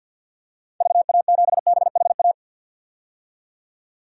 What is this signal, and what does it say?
Morse code, heard as a keyed tone, about 52 wpm, 700 Hz. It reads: VA7BHA
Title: DXCC-100 Call Signs - 6 Characters - 50wpm